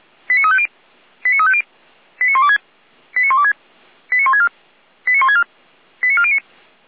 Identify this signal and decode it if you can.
Five-tone selective call (ZVEI), heard as quick runs of five tones, tones of about 70 ms. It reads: 89350 89350 89126 89126 89164 89164 89409